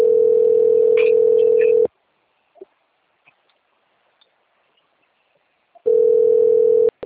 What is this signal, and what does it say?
Ringback tone (North America), 440 + 480 Hz.
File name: rg-1600-unknown- - - .15267